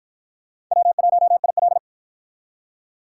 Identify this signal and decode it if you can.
Morse code, heard as a keyed tone, about 48 wpm, 700 Hz. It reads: W1IL